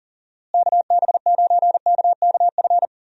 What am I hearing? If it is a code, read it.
Morse code, heard as a keyed tone, about 40 wpm, 700 Hz. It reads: KB9KKF
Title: USA Call Signs - 6 Characters - Repeated 2x - 40wpm